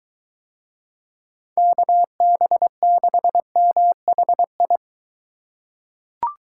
Morse code, heard as a keyed tone, about 23 wpm, 700 Hz. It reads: KB6MHI